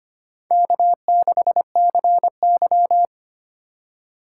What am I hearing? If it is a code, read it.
Morse code, heard as a keyed tone, about 25 wpm, 700 Hz. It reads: K6CY
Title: USA Call Signs - 4 Characters - 2x Character Spacing - 25wpm